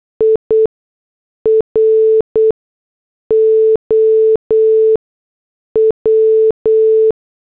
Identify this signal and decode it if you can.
Morse code, heard as a keyed tone, about 8 wpm, spaced as for 6 wpm, 440 Hz. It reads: IROW